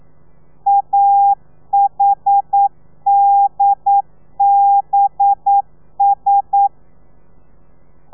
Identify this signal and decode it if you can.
Morse code, heard as a keyed tone, about 9 wpm, 800 Hz. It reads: AHDBS
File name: telegrafia / i2tfj / Lez4 40